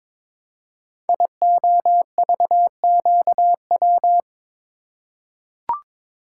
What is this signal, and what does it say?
Morse code, heard as a keyed tone, about 22 wpm, 700 Hz. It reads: IOVQW
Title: Code Groups - 5 Letter:Numbers - 22wpm